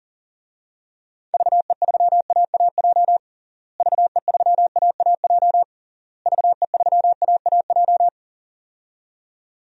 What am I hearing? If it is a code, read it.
Morse code, heard as a keyed tone, about 40 wpm, 700 Hz. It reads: VE3AAJ VE3AAJ VE3AAJ
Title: DXCC-100 Call Signs - 6 Characters - Repeated 3x - 40wpm